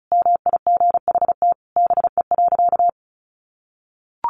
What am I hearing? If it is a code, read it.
Morse code, heard as a keyed tone, about 34 wpm, 700 Hz. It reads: MIGHT BE.
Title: Sentences from Top 1200 Words 35wpm